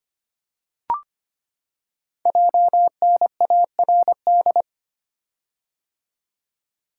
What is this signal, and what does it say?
Morse code, heard as a keyed tone, about 25 wpm, 700 Hz. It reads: JNARD
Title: Code Groups - 5 Letters - 25wpm